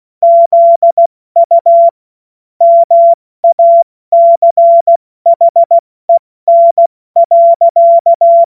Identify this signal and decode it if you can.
Morse code, heard as a keyed tone, about 16 wpm, 680 Hz. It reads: ZU MACHEN.
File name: Morsekurs/Witze